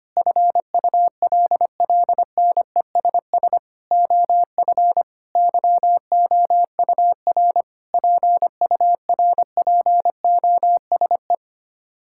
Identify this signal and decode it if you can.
Morse code, heard as a keyed tone, about 25 wpm, 700 Hz. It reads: FULLNESS OF YOUR PURPOSE
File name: Sets of 9 Words - 25wpm